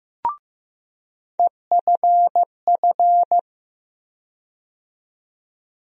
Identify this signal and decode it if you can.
Morse code, heard as a keyed tone, about 15 wpm, 700 Hz. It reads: EFF
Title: USA Call Sign Postfixes - 3 Letters 15wpm